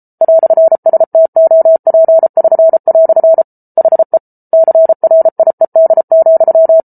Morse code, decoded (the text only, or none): "STOP<SN>" HE CRIED,